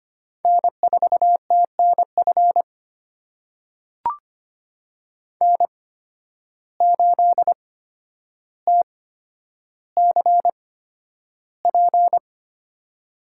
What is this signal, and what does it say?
Morse code, heard as a keyed tone, about 25 wpm, spaced as for 11 wpm, 700 Hz. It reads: N4TNF N8TCP
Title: USA Call Signs - 5 Characters - 8x Character Spacing - 25wpm